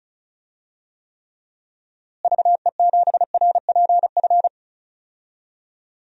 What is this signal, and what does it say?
Morse code, heard as a keyed tone, about 36 wpm, 700 Hz. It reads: VE7RPF
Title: DXCC-100 Call Signs - 6 Characters - Repeated 2x - 35wpm